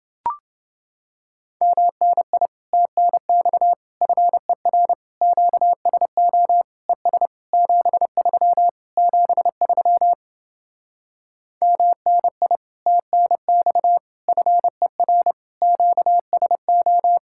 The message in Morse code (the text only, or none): MNI TNX FER QSO ES 73 73 MNI TNX FER QSO